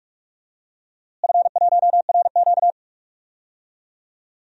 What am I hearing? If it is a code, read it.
Morse code, heard as a keyed tone, about 44 wpm, 700 Hz. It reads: F1RQ
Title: DXCC-100 Call Signs - 4 Characters - Repeated 2x - 45wpm